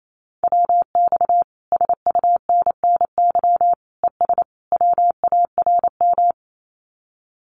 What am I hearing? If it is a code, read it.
Morse code, heard as a keyed tone, about 28 wpm, 700 Hz. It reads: WX SUNNY ES WARM
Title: Common QSO Phrases Encore 28wpm